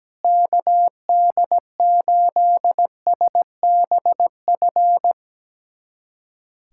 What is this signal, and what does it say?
Morse code, heard as a keyed tone, about 17 wpm, 700 Hz. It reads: KD8SBF
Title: USA Callsign DE Callsign - 6 Characters - 17wpm